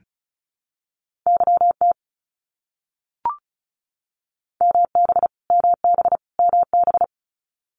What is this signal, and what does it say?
Morse code, heard as a keyed tone, about 34 wpm, 700 Hz. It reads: YT MB MB MB
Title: Canadian Provinces and Territories - Repeated 3x - 35wpm